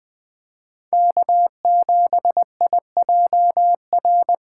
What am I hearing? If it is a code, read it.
Morse code, heard as a keyed tone, about 21 wpm, 700 Hz. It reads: K7IJR